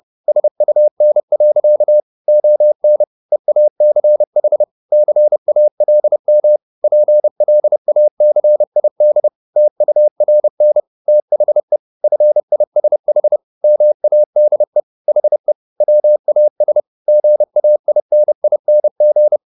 Morse code, heard as a keyed tone, about 30 wpm, 600 Hz. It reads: SUN. ON EACH CALM PLACID TURN THE FISH MADE HE WAS GAINING